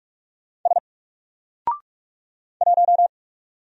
Morse code, heard as a keyed tone, about 46 wpm, 700 Hz. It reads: S 1
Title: Single Character (Letter:Number:Punctuation:Prosign) 45wpm